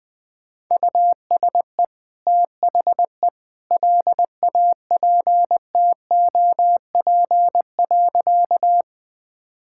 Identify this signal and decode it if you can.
Morse code, heard as a keyed tone, about 20 wpm, 700 Hz. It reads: USE THE LAPTOP.
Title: Easy Sentences with Hints - 20wpm